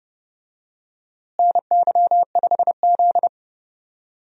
Morse code, heard as a keyed tone, about 30 wpm, 700 Hz. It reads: NY5Z
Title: USA Call Signs - 4 Characters - Repeated 3x - 30wpm